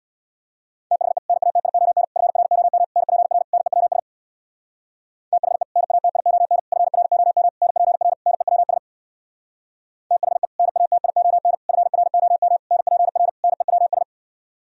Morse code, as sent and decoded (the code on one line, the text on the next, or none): - .... / -. .- - .. --- -. / ..-. .-. --- -- / -. .--. .-. / -. .-- ... / - .... / -. .- - .. --- -. / ..-. .-. --- -- / -. .--. .-. / -. .-- ... / - .... / -. .- - .. --- -. / ..-. .-. --- -- / -. .--. .-. / -. .-- ...
THE NATION FROM NPR NEWS THE NATION FROM NPR NEWS THE NATION FROM NPR NEWS